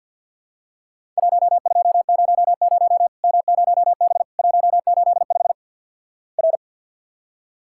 Morse code, heard as a keyed tone, about 50 wpm, 700 Hz. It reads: 1200 M0B 185 R
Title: Worked All Europe DX Contest - QTC Exchange - 50wpm